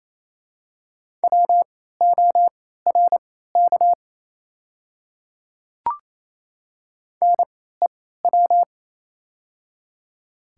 Morse code, heard as a keyed tone, about 28 wpm, spaced as for 16 wpm, 690 Hz. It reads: WORK NEW
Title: 3x Character Spacing - Top 200 Words - 28wpm